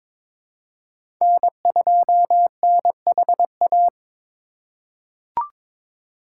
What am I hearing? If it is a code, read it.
Morse code, heard as a keyed tone, about 21 wpm, 700 Hz. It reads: N2NHA
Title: USA Call Signs - 2x Character Spacing - 22wpm